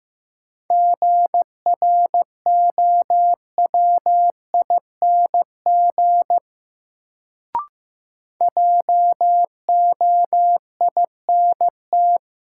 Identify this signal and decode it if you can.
Morse code, heard as a keyed tone, about 15 wpm, 700 Hz. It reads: GROWING JOINT
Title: Sets of 5 Words Encore - New Word Drill - 15wpm